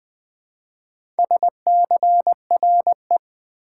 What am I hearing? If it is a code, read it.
Morse code, heard as a keyed tone, about 20 wpm, 700 Hz. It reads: SCRE
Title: Build A Word - 2 Letters At A Time - 6-Letter Words - 20wpm